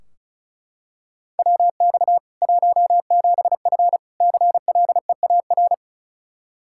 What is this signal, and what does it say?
Morse code, heard as a keyed tone, about 34 wpm, 700 Hz. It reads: WX 17F CLEAR
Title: Practice QSO - Exchange 2 - Info - 35wpm